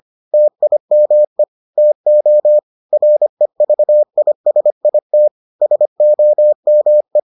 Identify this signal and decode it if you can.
Morse code, heard as a keyed tone, about 25 wpm, 600 Hz. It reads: TIME TO REVISIT SOME